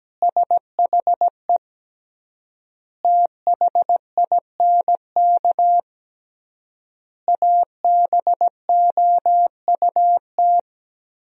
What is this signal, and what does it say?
Morse code, heard as a keyed tone, about 17 wpm, 700 Hz. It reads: SHE THINK ABOUT
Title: 3x Word-Spacing - Sentences from Top 200 Words - 17wpm